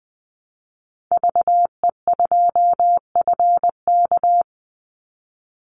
Morse code, heard as a keyed tone, about 20 wpm, 700 Hz. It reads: VE2FK
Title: Call Signs - 20wpm